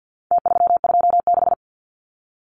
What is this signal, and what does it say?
Morse code, heard as a keyed tone, about 50 wpm, 700 Hz. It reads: T326
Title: Numbers - Contest Style - 4-Digit 50wpm